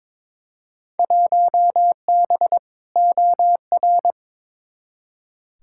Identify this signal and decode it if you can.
Morse code, heard as a keyed tone, about 22 wpm, 700 Hz. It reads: 1B OR
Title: Field Day Exchanges Encore - 22wpm